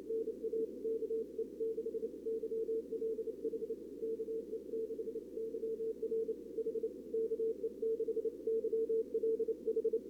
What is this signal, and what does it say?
Morse code, heard as a keyed tone, about 30 wpm, 440 Hz. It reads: NA KE6 YLH KE6 YR H KE6 YLH
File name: CWSprint / Feb2022 / k5tr-feb-na-sprint-cw-k5ot -0000